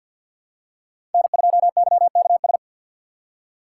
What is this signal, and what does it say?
Morse code, heard as a keyed tone, about 53 wpm, 700 Hz. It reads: N2YKS
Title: USA Call Signs - 5 Characters - 3x Character Spacing - 50wpm